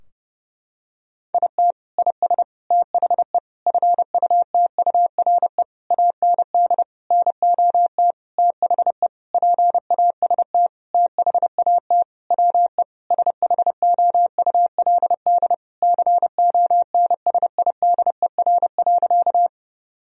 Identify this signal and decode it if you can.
Morse code, heard as a keyed tone, about 30 wpm, 700 Hz. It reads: IT IS THE FUTURE AND NOT THE PAST THAT WE SHOULD CONSIDER.
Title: Sentences from Top 1000 Words - Review Part 4 - 30wpm